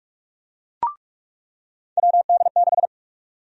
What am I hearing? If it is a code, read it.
Morse code, heard as a keyed tone, about 46 wpm, 700 Hz. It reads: WD6
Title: USA Call Sign Prefixes - 45wpm